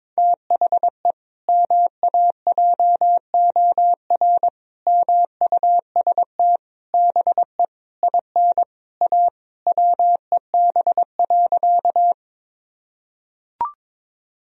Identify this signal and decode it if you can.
Morse code, heard as a keyed tone, about 22 wpm, 700 Hz. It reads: THE MAJOR MUST BE IN A WEB.
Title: Sentences from Top 1000 Words - Review Part 4 - 22wpm